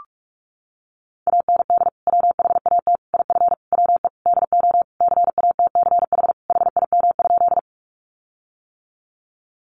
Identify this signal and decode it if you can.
Morse code, heard as a keyed tone, about 45 wpm, 700 Hz. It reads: AND WHAT IF WE DO CATCH HIM?
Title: Sentences from Top 1500 Words 45wpm